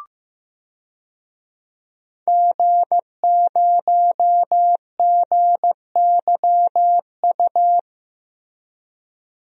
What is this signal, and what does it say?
Morse code, heard as a keyed tone, about 15 wpm, 700 Hz. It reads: G0GYU
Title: DXCC-100 Call Signs - 5 Characters - Repeated 3x - 15wpm